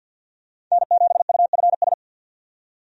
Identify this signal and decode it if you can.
Morse code, heard as a keyed tone, about 52 wpm, 700 Hz. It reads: N7UFS
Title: USA Call Signs - 5 Characters - 50wpm